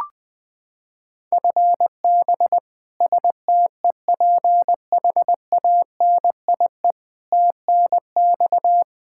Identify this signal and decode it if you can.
Morse code, heard as a keyed tone, about 20 wpm, 700 Hz. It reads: FB STEPHANIE TNX
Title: Practice QSO - Exchange 3 - More Info - 20wpm